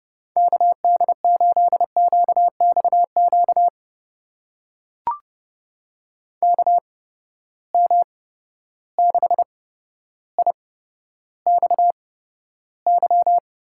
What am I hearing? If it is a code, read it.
Morse code, heard as a keyed tone, about 30 wpm, spaced as for 13 wpm, 700 Hz. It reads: KD8QXQ KM6IXY
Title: USA Call Signs - 6 Characters - 8x Character Spacing - 30wpm